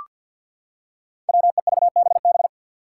Morse code, heard as a keyed tone, about 45 wpm, 700 Hz. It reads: WEVBB
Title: Five Challenging Letter-Number Combinations 50wpm